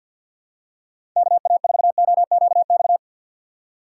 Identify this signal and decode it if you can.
Morse code, heard as a keyed tone, about 48 wpm, 700 Hz. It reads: KA4OQX
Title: USA Call Signs - 6 Characters - 4x Character Spacing - 50wpm